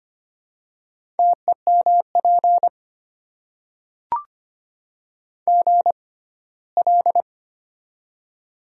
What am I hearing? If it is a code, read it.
Morse code, heard as a keyed tone, about 25 wpm, spaced as for 17 wpm, 700 Hz. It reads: TEMP GL
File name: Top 90 QSO Elements - 6x Character Spacing - 25wpm